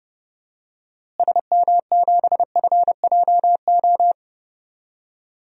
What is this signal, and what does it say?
Morse code, heard as a keyed tone, about 30 wpm, 700 Hz. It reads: SM7FJO
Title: DXCC-100 Call Signs - 6 Characters - Repeated 2x - 30wpm